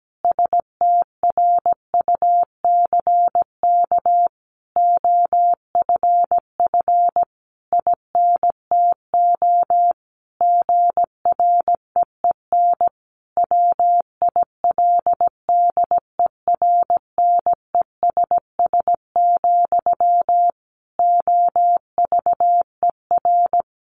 Morse code, heard as a keyed tone, about 17 wpm, 700 Hz. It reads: STRUCK OFF INTO GREEN WILDERNESS, OVER
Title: A Sound Of Thunder 17wpm